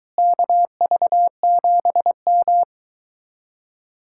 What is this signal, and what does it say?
Morse code, heard as a keyed tone, about 23 wpm, 700 Hz. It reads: KV7M